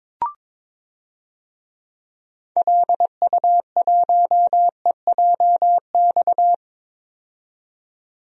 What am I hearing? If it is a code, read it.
Morse code, heard as a keyed tone, about 22 wpm, 700 Hz. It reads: LU1EJX
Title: DXCC-100 Call Signs - 6 Characters - 22wpm